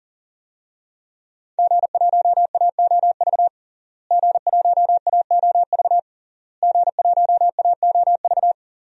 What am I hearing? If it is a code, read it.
Morse code, heard as a keyed tone, about 40 wpm, 700 Hz. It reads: G1AOV G1AOV G1AOV
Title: DXCC-100 Call Signs - 5 Characters - Repeated 3x - 40wpm